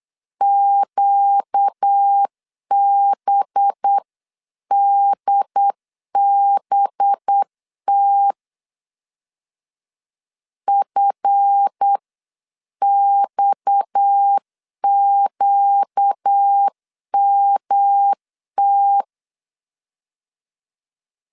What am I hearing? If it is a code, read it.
Morse code, heard as a keyed tone, about 8 wpm, 800 Hz. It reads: QBDBT FXQMT
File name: CW / DET / 1 9 1500C S A 8wpm 5637